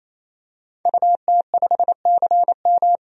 Morse code, heard as a keyed tone, about 27 wpm, 700 Hz. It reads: UT5CM
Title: DXCC-100 Call Signs - 5 Characters - 28wpm